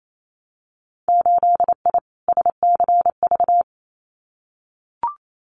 Morse code, heard as a keyed tone, about 28 wpm, 700 Hz. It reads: 8I SCV
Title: Winter Field Day Exchanges Encore 28wpm